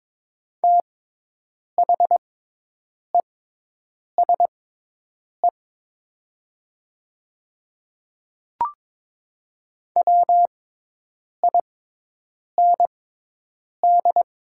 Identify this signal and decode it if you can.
Morse code, heard as a keyed tone, about 22 wpm, spaced as for 8 wpm, 700 Hz. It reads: THESE WIND
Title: 6x Character Spacing - Top 500 Words - 22wpm